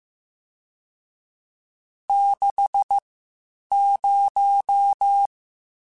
Morse code, heard as a keyed tone, about 15 wpm, spaced as for 8 wpm, 800 Hz. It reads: 60
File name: Spaced Characters / Spaced Characters-mp3s / Weather5 22 sp @13